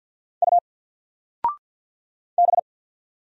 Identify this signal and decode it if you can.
Morse code, heard as a keyed tone, about 48 wpm, 700 Hz. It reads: U B
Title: Single Letter-Number Alt 50wpm